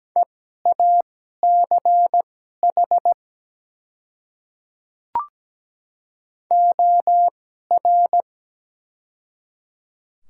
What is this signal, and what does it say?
Morse code, heard as a keyed tone, about 17 wpm, spaced as for 12 wpm, 700 Hz. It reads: EACH OR
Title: 2x Character Spacing - Top 200 Words - 17wpm